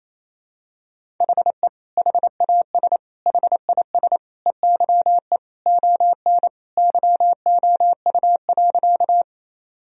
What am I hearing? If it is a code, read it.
Morse code, heard as a keyed tone, about 28 wpm, 700 Hz. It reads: HE HAS HIS EYE ON YOU.
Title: Sentences from Top 1000 Words - Review Part 1 - 28wpm